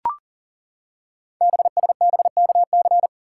Morse code, heard as a keyed tone, about 42 wpm, 690 Hz. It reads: BSBKC
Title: Code Groups - 5 Letters - 40wpm